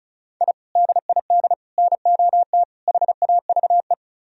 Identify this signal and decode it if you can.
Morse code, heard as a keyed tone, about 46 wpm, 700 Hz. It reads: I DID NOT HAVE